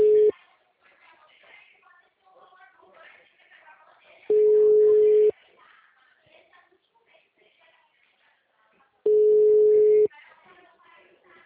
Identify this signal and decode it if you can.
Ringback tone (Europe, 425 Hz), with speech.
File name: out- -2021- - - .22890